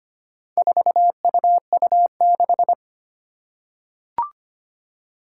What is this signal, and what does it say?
Morse code, heard as a keyed tone, about 25 wpm, 700 Hz. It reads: 4UU6
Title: Four Challenging Letter-Number Combinations 25wpm